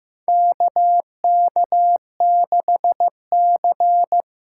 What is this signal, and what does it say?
Morse code, heard as a keyed tone, about 15 wpm, 700 Hz. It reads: KK6C
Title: USA Call Signs - 6 Characters - 8x Character Spacing - 15wpm